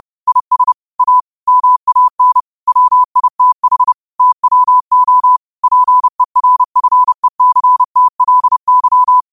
Morse code, heard as a keyed tone, about 30 wpm, 1000 Hz. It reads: IS A MAN WITH TWO PERFECTLY
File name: qotdfiles / 30wpm